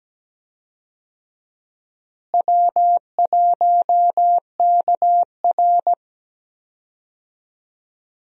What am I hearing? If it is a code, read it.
Morse code, heard as a keyed tone, about 17 wpm, 700 Hz. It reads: W1KR
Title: USA Call Signs - 4 Characters - Repeated 3x - 17wpm